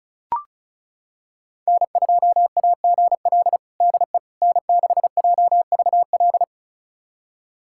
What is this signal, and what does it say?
Morse code, heard as a keyed tone, about 34 wpm, 700 Hz. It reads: N2AGL DE N6JVL
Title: USA Callsign DE Callsign - 5 Characters - 35wpm